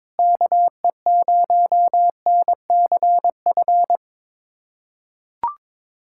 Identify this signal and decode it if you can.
Morse code, heard as a keyed tone, about 22 wpm, 700 Hz. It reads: KE0NCF
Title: USA Callsign DE Callsign 22wpm